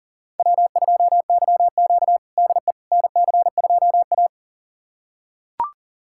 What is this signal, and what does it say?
Morse code, heard as a keyed tone, about 40 wpm, 700 Hz. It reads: W2YQ DE NC2A
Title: USA Callsign DE Callsign - 4 Characters - 40wpm